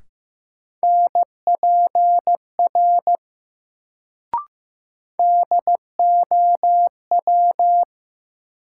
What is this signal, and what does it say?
Morse code, heard as a keyed tone, about 15 wpm, 700 Hz. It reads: NPR DOW
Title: Sets of 5 Words - New Word Drill - 15wpm